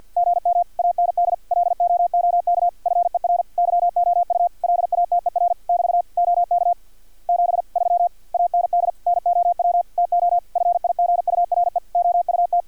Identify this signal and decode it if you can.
Morse code, heard as a keyed tone, about 50 wpm, 700 Hz. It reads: GM AND LOOK FER YOU LATER <BT> OK 73 AND NOW TO FIGURE OUT
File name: archived / CRSnet